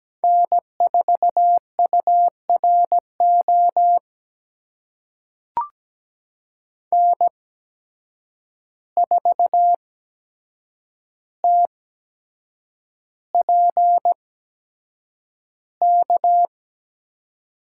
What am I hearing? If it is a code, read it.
Morse code, heard as a keyed tone, about 17 wpm, spaced as for 7 wpm, 700 Hz. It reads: N4URO N4TPK
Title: USA Call Signs - 5 Characters - 8x Character Spacing - 17wpm